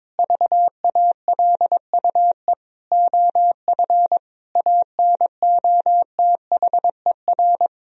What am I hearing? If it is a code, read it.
Morse code, heard as a keyed tone, about 22 wpm, 700 Hz. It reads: VALUE OF ANOTHER
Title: Sentences from Top 1000 Words - Review Part 3 - 22wpm